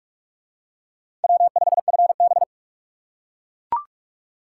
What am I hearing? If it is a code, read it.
Morse code, heard as a keyed tone, about 44 wpm, 700 Hz. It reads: W5FB